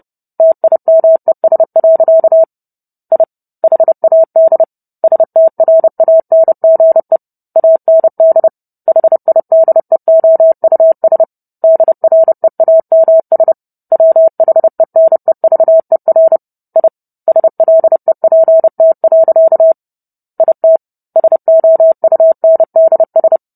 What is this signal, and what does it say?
Morse code, heard as a keyed tone, about 30 wpm, 650 Hz. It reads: TIMES. I HAD STRANGE AND HIDEOUS DREAMS WHENEVER I SLEPT. IT SOUNDS